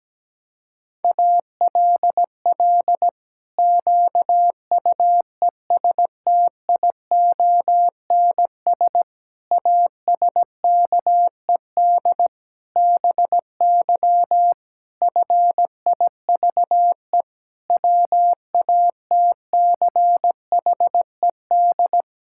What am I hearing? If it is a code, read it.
Morse code, heard as a keyed tone, about 17 wpm, 700 Hz. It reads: ALL QUESTIONS ASKED BY FIVE WATCHED
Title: Pangram Challenge v2 - 17wpm